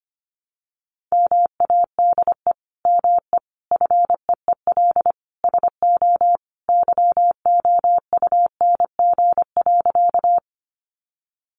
Morse code, heard as a keyed tone, about 25 wpm, 700 Hz. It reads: MADE ME FEEL SO YOUNG.
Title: Sentences from Top 400 Words 25wpm